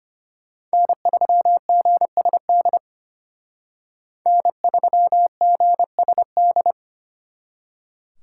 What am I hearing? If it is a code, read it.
Morse code, heard as a keyed tone, about 32 wpm, 700 Hz. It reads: N3GSD N3GSD